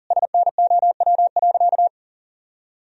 Morse code, heard as a keyed tone, about 40 wpm, 700 Hz. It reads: SNOW.
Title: Sentences from Top 1000 Words - Review Part 3 - 40wpm